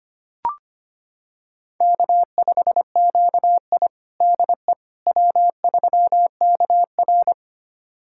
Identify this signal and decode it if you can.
Morse code, heard as a keyed tone, about 25 wpm, 700 Hz. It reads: K5QI DE W3KR
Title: USA Callsign DE Callsign - 4 Characters - 25wpm